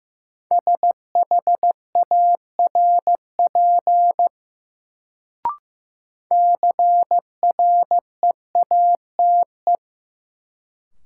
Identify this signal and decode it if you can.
Morse code, heard as a keyed tone, about 15 wpm, 700 Hz. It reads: SHARP CREATE
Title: Sentences from Top 1600 Words - New Word Drill Alt - 15wpm